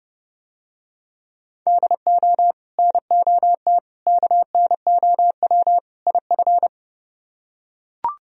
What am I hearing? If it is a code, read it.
Morse code, heard as a keyed tone, about 29 wpm, 700 Hz. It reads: DO NOT KNOW IF